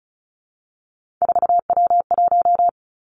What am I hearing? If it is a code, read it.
Morse code, heard as a keyed tone, about 36 wpm, 700 Hz. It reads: 4W1
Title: Three Challenging Letter-Number Combinations 35wpm